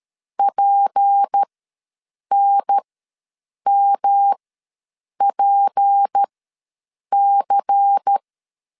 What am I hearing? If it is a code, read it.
Morse code, heard as a keyed tone, about 13 wpm, spaced as for 7 wpm, 800 Hz. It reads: PNMPC
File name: LEZIONI / 6 PKX0 / 6 P ws 12wpm 3117